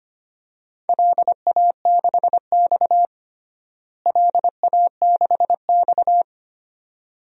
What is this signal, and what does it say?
Morse code, heard as a keyed tone, about 25 wpm, 700 Hz. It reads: LA6X LA6X
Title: DXCC-100 Call Signs - 4 Characters - Repeated 2x - 25wpm